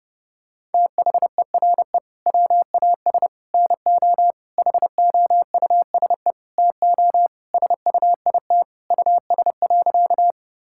Morse code, heard as a keyed tone, about 30 wpm, 700 Hz. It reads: THERE WAS NO HOUSE TO SUIT US.
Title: Sentences from Top 1600 Words 30wpm